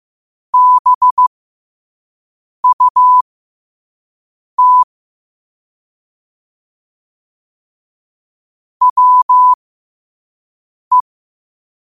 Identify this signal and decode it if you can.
Morse code, heard as a keyed tone, about 15 wpm, spaced as for 5 wpm, 1000 Hz. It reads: BUT WE
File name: 5wpm